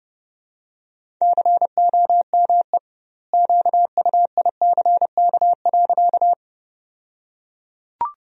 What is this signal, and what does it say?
Morse code, heard as a keyed tone, about 30 wpm, 700 Hz. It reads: COME QUICK.
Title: 2x Word-Spacing - Sentences from Top 500 Words - 30wpm